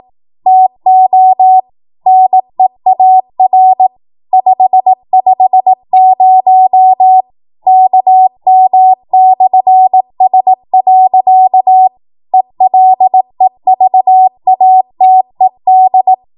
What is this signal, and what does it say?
Morse code, heard as a keyed tone, about 18 wpm, 750 Hz. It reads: TO NEAR 550 KM/S. ELEVATED
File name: Morse / 18WPM